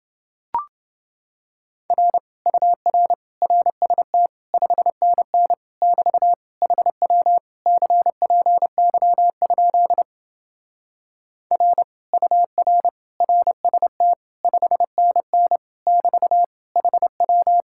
Morse code, heard as a keyed tone, about 31 wpm, 700 Hz. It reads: R UR RST 5NN <BT> HW CPY? R UR RST 5NN <BT> HW